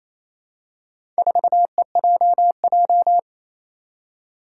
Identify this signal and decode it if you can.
Morse code, heard as a keyed tone, about 28 wpm, 700 Hz. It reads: 4EJJ
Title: Four Challenging Letter-Number Combinations 28wpm